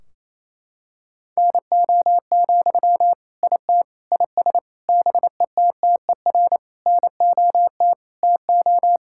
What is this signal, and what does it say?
Morse code, heard as a keyed tone, about 28 wpm, 700 Hz. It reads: NO, IT IS BETTER NOT TO
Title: Sentences from Top 1600 Words 28wpm